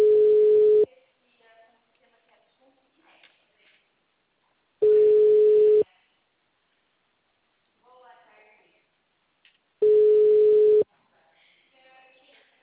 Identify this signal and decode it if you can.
Ringback tone (Europe, 425 Hz), with speech.
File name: out- -2044